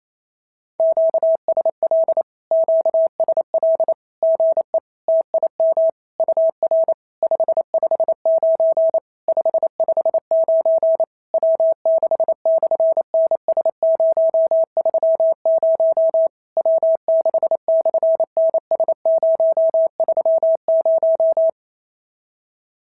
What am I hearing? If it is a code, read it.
Morse code, heard as a keyed tone, about 28 wpm, 690 Hz. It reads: QSL QSL GE TIM UR 559 559 W6/NS030 W6/NS030
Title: SOTA QSO Practice 28wpm